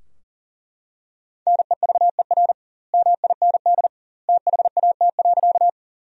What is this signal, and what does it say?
Morse code, heard as a keyed tone, about 40 wpm, 700 Hz. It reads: NEVER MIND THAT.
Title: 2x Word-Spacing - Sentences from Top 500 Words - 40wpm